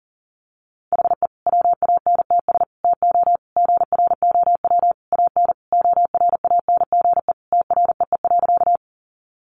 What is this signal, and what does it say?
Morse code, heard as a keyed tone, about 40 wpm, 700 Hz. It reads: HE WANTS TO GROW AN ORANGE TREE.
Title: Easy Phrases 40wpm